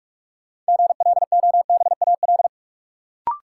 Morse code, heard as a keyed tone, about 47 wpm, 700 Hz. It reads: GLOBAL
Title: Sentences from Top 1200 Words - New Word Drill - 45wpm